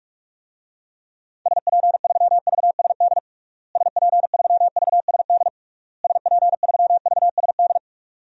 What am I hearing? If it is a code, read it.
Morse code, heard as a keyed tone, about 47 wpm, 700 Hz. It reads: SP3VSD SP3VSD SP3VSD
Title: DXCC-100 Call Signs - 6 Characters - Repeated 3x - 45wpm